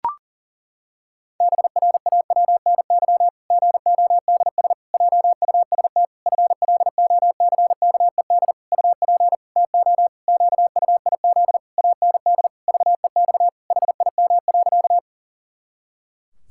Morse code, heard as a keyed tone, about 40 wpm, 700 Hz. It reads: BRAWNY GODS JUST FLOCKED UP TO QUIZ AND VEX HIM.
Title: Pangram Challenge v2 - 40wpm